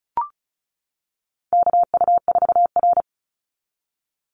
Morse code, heard as a keyed tone, about 34 wpm, 700 Hz. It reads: KU4R
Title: NA SOTA Chaser Callsigns 35wpm